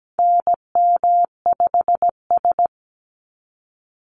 Morse code, heard as a keyed tone, about 17 wpm, 700 Hz. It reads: NM5S
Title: NA SOTA Chaser Callsigns 17wpm